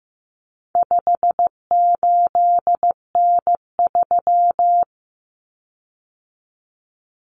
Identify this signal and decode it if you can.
Morse code, heard as a keyed tone, about 15 wpm, 700 Hz. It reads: 58N3
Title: Common Cut Numbers - 4-Digit 15wpm